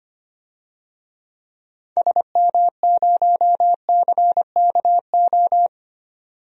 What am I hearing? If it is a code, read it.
Morse code, heard as a keyed tone, about 25 wpm, 700 Hz. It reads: SM0CKO
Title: DXCC-100 Call Signs - 6 Characters - 25wpm